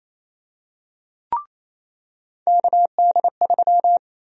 Morse code, heard as a keyed tone, about 28 wpm, 700 Hz. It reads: KD3
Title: USA Call Sign Prefixes - 28wpm